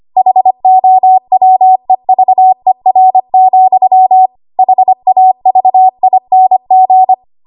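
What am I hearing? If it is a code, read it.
Morse code, heard as a keyed tone, about 25 wpm, 750 Hz. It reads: HOWEVER, HAVING
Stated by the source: ARRL 25 WPM code practice archive.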